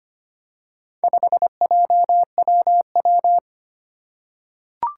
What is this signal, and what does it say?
Morse code, heard as a keyed tone, about 25 wpm, 700 Hz. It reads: 5JWW
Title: Four Challenging Letter-Number Combinations 25wpm